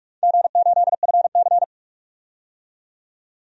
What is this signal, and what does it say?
Morse code, heard as a keyed tone, about 46 wpm, 700 Hz. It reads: G8FC
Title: DXCC-100 Call Signs - 4 Characters - 45wpm